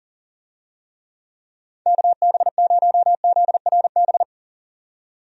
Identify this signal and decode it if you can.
Morse code, heard as a keyed tone, about 38 wpm, 700 Hz. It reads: KB0ZRB
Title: USA Call Signs - 6 Characters - 40wpm